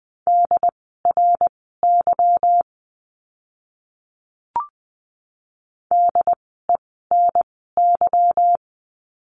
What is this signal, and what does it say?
Morse code, heard as a keyed tone, about 20 wpm, spaced as for 15 wpm, 700 Hz. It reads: DRY DENY
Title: 146 Lesson 24f - Letter y - Words 20wpm